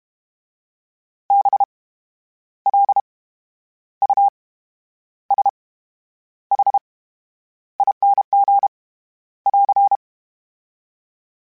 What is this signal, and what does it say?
Morse code, heard as a keyed tone, about 32 wpm, spaced as for 10 wpm, 800 Hz. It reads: BLUSHING <AR>
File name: Special INGs / Special INGs-mp3s / VerySlowToFaster Ing @30